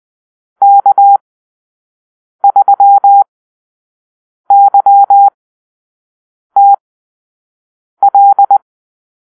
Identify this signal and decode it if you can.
Morse code, heard as a keyed tone, about 20 wpm, spaced as for 6 wpm, 800 Hz. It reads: K3YTL
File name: arrl / Call-Sign Drill-5